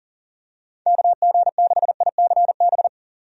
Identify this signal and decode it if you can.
Morse code, heard as a keyed tone, about 40 wpm, 700 Hz. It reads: KG6ICB
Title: USA Call Signs - 6 Characters - 40wpm